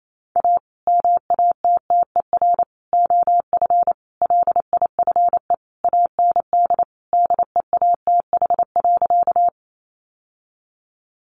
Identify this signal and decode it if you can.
Morse code, heard as a keyed tone, about 28 wpm, 700 Hz. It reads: A MATTER OF LIFE AND DEATH.
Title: Sentences from Top 1100 Words 28wpm